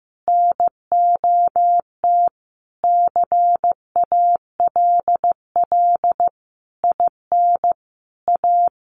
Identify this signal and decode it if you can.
Morse code, heard as a keyed tone, about 15 wpm, 700 Hz. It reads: NOT CALL IN A
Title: Sentences from Top 1500 Words 15wpm